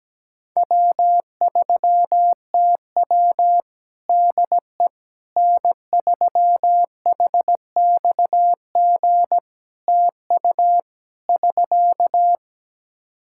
Morse code, heard as a keyed tone, about 17 wpm, 700 Hz. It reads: W3TW DE N3HXG TU <SK>
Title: Practice QSO - Exchange 4 - Ending - 17wpm